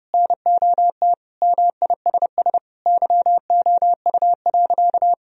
Morse code, heard as a keyed tone, about 30 wpm, 700 Hz. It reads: NOT MISS YOU.
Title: Sentences from Top 1000 Words - Review Part 3 - 30wpm